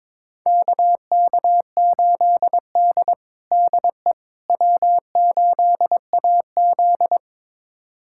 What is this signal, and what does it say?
Morse code, heard as a keyed tone, about 22 wpm, 700 Hz. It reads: KK8D DE W8AZ
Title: USA Callsign DE Callsign - 4 Characters - 22wpm